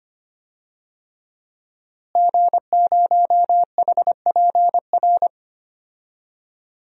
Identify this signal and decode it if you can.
Morse code, heard as a keyed tone, about 25 wpm, 700 Hz. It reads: G0HPR
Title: DXCC-100 Call Signs - 5 Characters - Repeated 3x - 25wpm